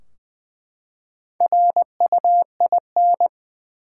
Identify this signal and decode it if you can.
Morse code, heard as a keyed tone, about 20 wpm, 700 Hz. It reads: RUIN
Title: Four Letter Words - 20wpm